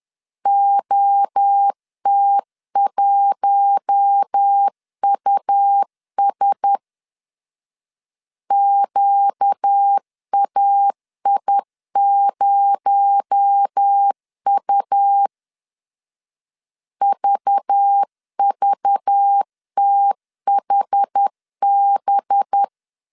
Morse code, heard as a keyed tone, about 10 wpm, 800 Hz. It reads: OT1US QAI0U VVTHB